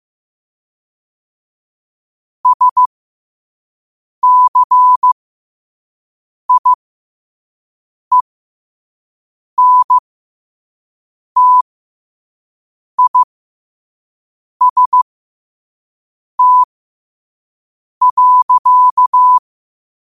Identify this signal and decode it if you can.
Morse code, heard as a keyed tone, about 15 wpm, spaced as for 5 wpm, 1000 Hz. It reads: SCIENTIST.
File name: qotdfiles / 5wpm